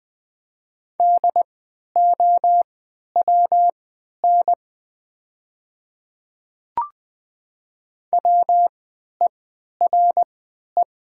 Morse code, heard as a keyed tone, about 20 wpm, spaced as for 11 wpm, 700 Hz. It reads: DOWN WERE
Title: 3x Character Spacing - Top 100 Words - 20wpm